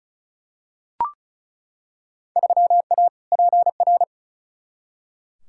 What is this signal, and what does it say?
Morse code, heard as a keyed tone, about 36 wpm, 700 Hz. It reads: 3A PR
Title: Field Day Exchanges Encore - 35wpm